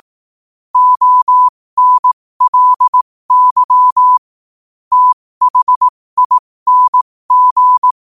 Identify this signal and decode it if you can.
Morse code, heard as a keyed tone, about 18 wpm, spaced as for 15 wpm, 1000 Hz. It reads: ONLY THING
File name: qotdfiles / 15wpm